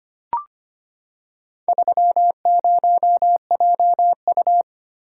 Morse code, heard as a keyed tone, about 25 wpm, 700 Hz. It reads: 30JU
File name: practice / 25wpm / Four Letter-Number Combinations - 25wpm